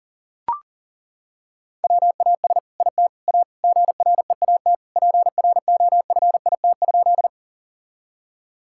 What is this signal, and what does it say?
Morse code, heard as a keyed tone, about 40 wpm, 690 Hz. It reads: WAS IT A GREAT PROFIT?
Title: Sentences from Top 1000 Words - Review Part 5 - 40wpm